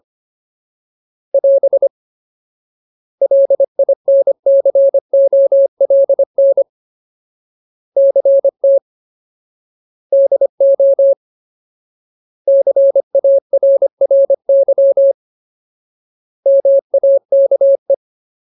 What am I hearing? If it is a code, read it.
Morse code, heard as a keyed tone, about 25 wpm, 550 Hz. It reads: <AS> LINCOLN CT DO CARRY MAKE
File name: Ultimate Word List / 25WPM Ultimate